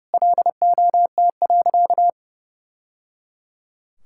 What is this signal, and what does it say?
Morse code, heard as a keyed tone, about 30 wpm, 700 Hz. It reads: LOT.
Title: Sentences from Top 1000 Words - Review Part 5 - 30wpm